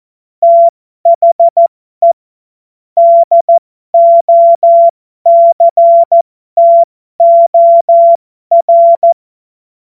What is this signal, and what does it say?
Morse code, heard as a keyed tone, about 14 wpm, spaced as for 12 wpm, 680 Hz. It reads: THE DOCTOR
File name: Morsekurs / Witze